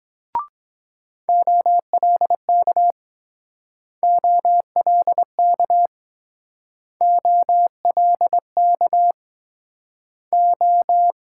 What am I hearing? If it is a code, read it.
Morse code, heard as a keyed tone, about 22 wpm, 700 Hz. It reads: OLK OLK OLK O